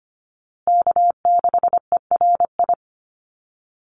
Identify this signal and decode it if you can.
Morse code, heard as a keyed tone, about 25 wpm, 700 Hz. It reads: K6ERI
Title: USA Call Signs Repeated 3x 25wpm